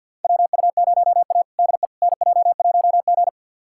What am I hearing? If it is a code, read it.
Morse code, heard as a keyed tone, about 47 wpm, 690 Hz. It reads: WU0A DE NJ1G
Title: USA Callsign DE Callsign - 4 Characters - 50wpm